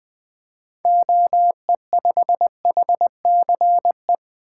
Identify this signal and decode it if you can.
Morse code, heard as a keyed tone, about 20 wpm, 700 Hz. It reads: OE5HCE
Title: 2023 SOTA Activator Call Signs - 20wpm